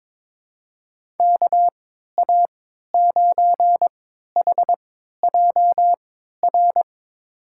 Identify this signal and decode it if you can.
Morse code, heard as a keyed tone, about 22 wpm, spaced as for 12 wpm, 700 Hz. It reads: KA9HJR
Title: USA Call Signs - 6 Characters - 3x Character Spacing - 22wpm